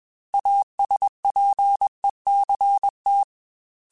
Morse code, heard as a keyed tone, about 21 wpm, 800 Hz. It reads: ASPECT
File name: Technical Subjects / Technical Subjects-mp3s / Sw Pwr Sup @20